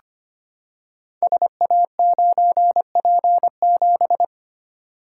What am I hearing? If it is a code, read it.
Morse code, heard as a keyed tone, about 25 wpm, 700 Hz. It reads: SA9P7
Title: Code Groups - 5 Letter:Numbers - 25wpm